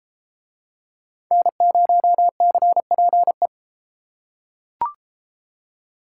Morse code, heard as a keyed tone, about 32 wpm, 700 Hz. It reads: N0CPE